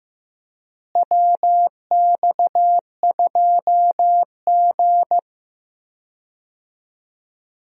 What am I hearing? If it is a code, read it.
Morse code, heard as a keyed tone, about 15 wpm, 700 Hz. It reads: WX2G
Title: USA Call Signs - 4 Characters - Repeated 2x - 15wpm